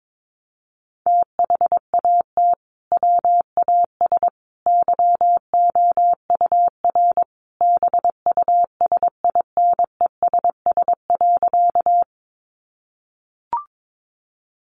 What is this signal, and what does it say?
Morse code, heard as a keyed tone, about 22 wpm, 700 Hz. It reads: THAT WAS YOUR BUSINESS.
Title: Sentences from Top 600 Words 22wpm